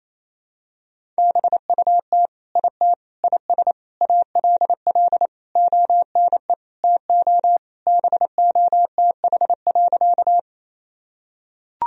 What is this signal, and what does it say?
Morse code, heard as a keyed tone, about 27 wpm, 700 Hz. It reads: BUT IT IS ALL ONE TO BOTH.
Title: Sentences from Top 1000 Words - Review Part 1 - 28wpm